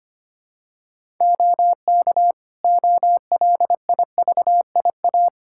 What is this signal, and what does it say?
Morse code, heard as a keyed tone, about 25 wpm, 700 Hz. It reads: OK OLIVIA
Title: Practice QSO - Exchange 2 - Info - 25wpm